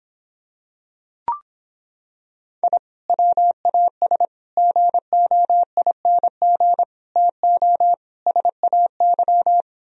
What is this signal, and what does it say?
Morse code, heard as a keyed tone, about 26 wpm, 700 Hz. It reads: I WAS GOING TO SAY